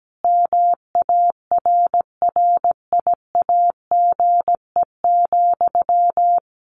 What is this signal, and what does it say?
Morse code, heard as a keyed tone, about 17 wpm, 700 Hz. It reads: MARRIAGE,
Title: Sentences from Top 1100 Words 17wpm